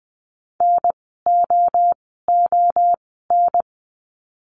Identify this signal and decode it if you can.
Morse code, heard as a keyed tone, about 20 wpm, spaced as for 15 wpm, 700 Hz. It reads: NOON
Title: 054 Lesson 10f - Letter r - Words 20wpm